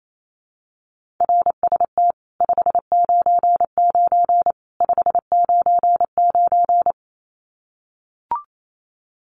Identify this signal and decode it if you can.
Morse code, heard as a keyed tone, about 28 wpm, 700 Hz. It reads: RST 599 599
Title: Common QSO Phrases 28wpm